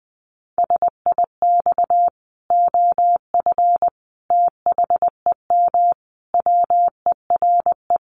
Morse code, heard as a keyed tone, about 20 wpm, 700 Hz. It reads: SIX OF THEM WERE
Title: Sentences from Top 1100 Words 20wpm